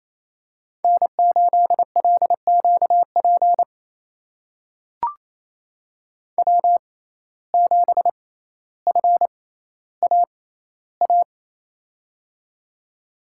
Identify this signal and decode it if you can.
Morse code, heard as a keyed tone, about 28 wpm, spaced as for 14 wpm, 700 Hz. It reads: N8LQP W7FAA
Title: USA Call Signs - 5 Characters - 6x Character Spacing - 28wpm